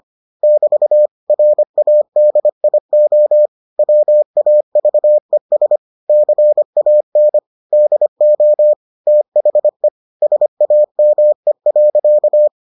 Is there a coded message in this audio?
Yes: Morse code from 0.4 to 12.5 s, heard as a keyed tone, about 25 wpm, 600 Hz. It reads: <BT> RADIO WAVES CAN DO THE SAME.